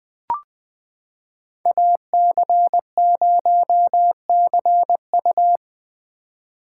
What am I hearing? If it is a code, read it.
Morse code, heard as a keyed tone, about 20 wpm, 700 Hz. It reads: AC0CU
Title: 2023 POTA Hunter Call Signs - 20wpm